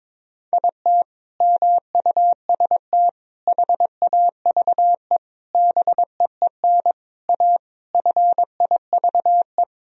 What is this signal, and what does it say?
Morse code, heard as a keyed tone, about 22 wpm, 700 Hz. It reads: IT MUST HAVE BEEN A FIVE
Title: Sentences from Top 1000 Words - Review Part 2 - 22wpm